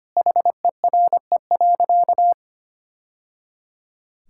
Morse code, heard as a keyed tone, about 25 wpm, 700 Hz. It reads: HERE.
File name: practice / 25wpm / Sentences with 3-Word Phrases - 25wpm